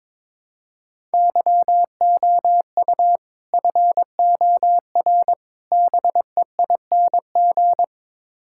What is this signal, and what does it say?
Morse code, heard as a keyed tone, about 22 wpm, 700 Hz. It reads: YOU FOR BEING
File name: practice / 22wpm / Build A Phrase - 5-Word Phrases - 22wpm